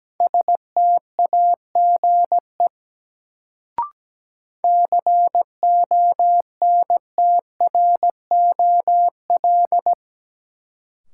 Morse code, heard as a keyed tone, about 17 wpm, 700 Hz. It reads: STAGE CONTROL
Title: Sentences from Top 1200 Words - New Word Drill Alt - 17wpm